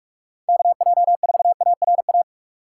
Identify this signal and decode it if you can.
Morse code, heard as a keyed tone, about 45 wpm, 700 Hz. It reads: KJ4ARA